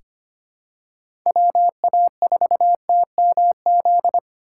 Morse code, heard as a keyed tone, about 25 wpm, 700 Hz. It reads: WA4TMZ
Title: USA Call Signs - 3x Character Spacing - 25wpm